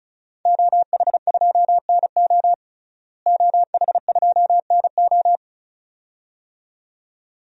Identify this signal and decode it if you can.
Morse code, heard as a keyed tone, about 36 wpm, 700 Hz. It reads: OH2NO OH2NO
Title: DXCC-100 Call Signs - 5 Characters - Repeated 2x - 35wpm